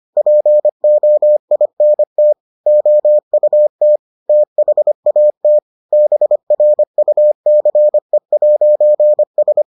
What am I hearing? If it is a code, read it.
Morse code, heard as a keyed tone, about 25 wpm, 600 Hz. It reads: POINT OUT THAT BRUCE'S